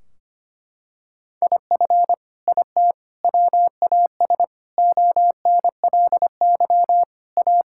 Morse code, heard as a keyed tone, about 25 wpm, 700 Hz. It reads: IF IT WAS ONLY A
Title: Sentences from Top 1600 Words 25wpm